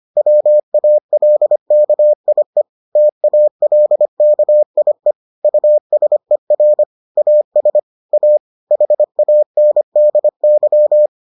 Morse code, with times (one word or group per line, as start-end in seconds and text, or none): 0.2-2.6 WALKIE
3.0-5.1 TALKIE
5.4-6.8 USER
7.2-7.8 AS
8.1-8.4 A
8.7-11.1 HANDY